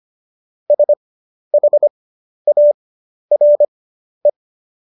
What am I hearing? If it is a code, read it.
Morse code, heard as a keyed tone, about 25 wpm, spaced as for 11 wpm, 590 Hz. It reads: SHARE
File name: Jokes-25-char-10-eff